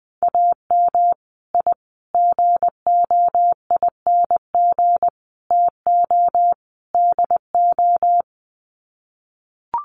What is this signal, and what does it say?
Morse code, heard as a keyed tone, about 20 wpm, 700 Hz. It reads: AM I GOING TO DO
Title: Sets of 5 Words Encore 20wpm